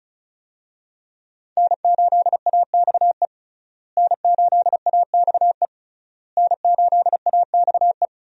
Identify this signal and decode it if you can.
Morse code, heard as a keyed tone, about 36 wpm, 700 Hz. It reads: N8AXE N8AXE N8AXE
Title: USA Call Signs - 5 Characters - Repeated 3x - 35wpm